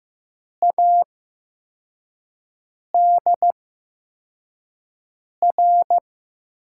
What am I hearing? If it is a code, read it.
Morse code, heard as a keyed tone, about 15 wpm, spaced as for 4 wpm, 700 Hz. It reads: ADR
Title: USA Call Signs - 8x Character Spacing - 15wpm